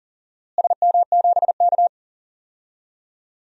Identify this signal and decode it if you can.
Morse code, heard as a keyed tone, about 40 wpm, 700 Hz. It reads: SM7K
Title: DXCC-100 Call Signs - 4 Characters - Repeated 2x - 40wpm